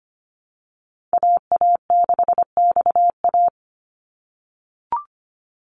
Morse code, heard as a keyed tone, about 25 wpm, 700 Hz. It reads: AA6XA
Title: NA SOTA Activator Callsigns 25wpm